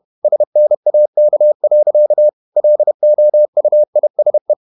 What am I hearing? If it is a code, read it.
Morse code, heard as a keyed tone, about 31 wpm, 600 Hz. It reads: SNAK. LOUISE
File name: Ferie QRQ 31wpm